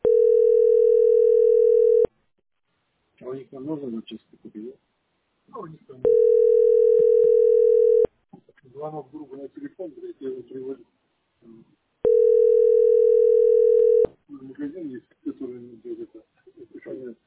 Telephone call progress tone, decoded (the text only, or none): ringback tone